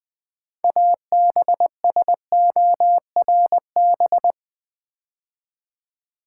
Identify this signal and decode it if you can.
Morse code, heard as a keyed tone, about 20 wpm, 700 Hz. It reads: ABSORB
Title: Build A Word - 3 Letters At A Time - 6-Letter Words - 20wpm